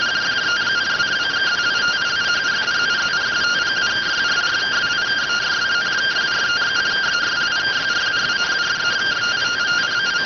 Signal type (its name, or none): two-tone teleprinter signal